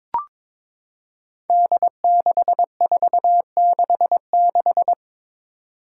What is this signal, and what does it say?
Morse code, heard as a keyed tone, about 22 wpm, 700 Hz. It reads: D6466
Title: Five Challenging Letter-Number Combinations 22wpm